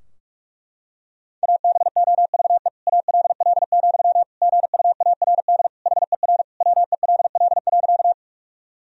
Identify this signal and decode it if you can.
Morse code, heard as a keyed tone, about 46 wpm, 700 Hz. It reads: ABOVE ALL, GUARD HER WELL.
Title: Sentences from Top 1000 Words - Review Part 4 - 45wpm